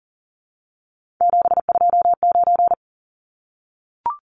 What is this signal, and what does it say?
Morse code, heard as a keyed tone, about 40 wpm, 700 Hz. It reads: 729
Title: Numbers - 3-Digit 40wpm